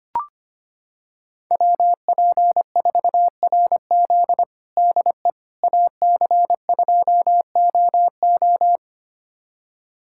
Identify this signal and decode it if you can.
Morse code, heard as a keyed tone, about 25 wpm, 700 Hz. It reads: WP4RZ DE AC2OO
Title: USA Callsign DE Callsign - 5 Characters - 25wpm